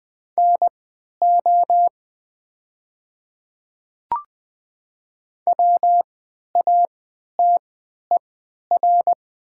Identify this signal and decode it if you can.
Morse code, heard as a keyed tone, about 20 wpm, spaced as for 11 wpm, 700 Hz. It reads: NO WATER
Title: 3x Character Spacing - Top 100 Words - 20wpm